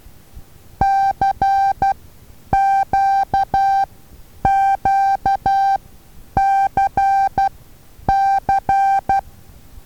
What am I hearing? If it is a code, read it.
Morse code, heard as a keyed tone, about 12 wpm, spaced as for 9 wpm, 780 Hz. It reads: CQQCC
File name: Taviro HA6VA A